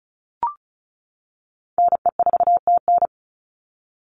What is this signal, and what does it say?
Morse code, heard as a keyed tone, about 34 wpm, 700 Hz. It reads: NE4TN
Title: NA SOTA Chaser Callsigns 35wpm